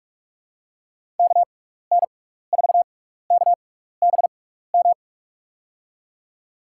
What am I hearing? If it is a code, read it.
Morse code, heard as a keyed tone, about 45 wpm, spaced as for 15 wpm, 700 Hz. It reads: KN4KBM
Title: USA Call Signs - 6x Character Spacing - 45wpm